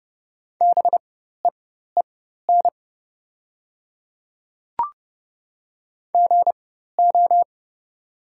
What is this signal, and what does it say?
Morse code, heard as a keyed tone, about 30 wpm, spaced as for 14 wpm, 700 Hz. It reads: BEEN GO
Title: 4x Character Spacing - Top 200 Words - 30wpm